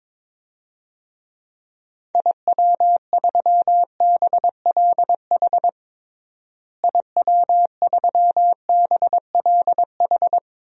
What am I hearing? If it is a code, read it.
Morse code, heard as a keyed tone, about 22 wpm, 700 Hz. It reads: IW3BLH IW3BLH
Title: DXCC-100 Call Signs - 6 Characters - Repeated 3x - 22wpm